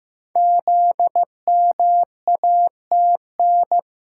Morse code, heard as a keyed tone, about 15 wpm, 700 Hz. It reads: ZMATN
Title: Code Groups - 5 Letters - 15wpm